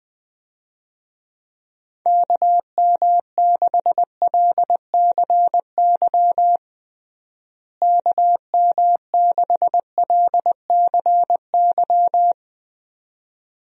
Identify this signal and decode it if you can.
Morse code, heard as a keyed tone, about 21 wpm, 700 Hz. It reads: KM6LCY KM6LCY
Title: USA Call Signs - 6 Characters - Repeated 2x - 20wpm